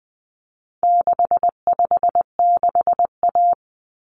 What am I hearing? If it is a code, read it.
Morse code, heard as a keyed tone, about 20 wpm, 700 Hz. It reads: 656A
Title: Common Cut Numbers - 4-Digit 20wpm